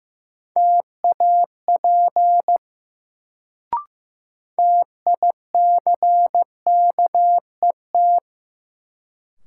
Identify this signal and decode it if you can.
Morse code, heard as a keyed tone, about 15 wpm, 700 Hz. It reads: TAP TICKET
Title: Sentences from Top 1400 Words - New Word Drill - 15wpm